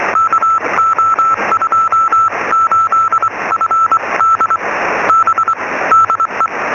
Morse code, heard as a keyed tone, about 23 wpm, spaced as for 20 wpm, 1310 Hz, with noise. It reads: KO28FD B DE
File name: sm7lcb / 0600 es8a 23cm